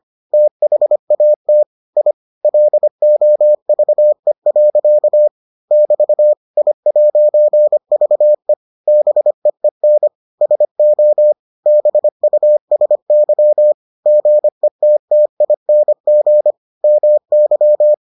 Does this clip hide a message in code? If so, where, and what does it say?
Yes: Morse code from 0.3 to 17.9 s, heard as a keyed tone, about 25 wpm, 600 Hz. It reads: THAT I LOVE. <BT> I'VE BEEN SO BUSY GETTING MY